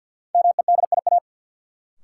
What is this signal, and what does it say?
Morse code, heard as a keyed tone, about 51 wpm, 700 Hz. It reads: MEDIA
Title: Sentences from Top 1000 Words - Review Part 4 - New Word Drill - 50wpm